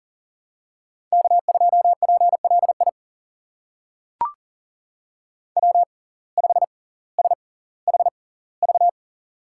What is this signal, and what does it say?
Morse code, heard as a keyed tone, about 40 wpm, spaced as for 20 wpm, 700 Hz. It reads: K2PLI W5SHV
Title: USA Call Signs - 5 Characters - 6x Character Spacing - 40wpm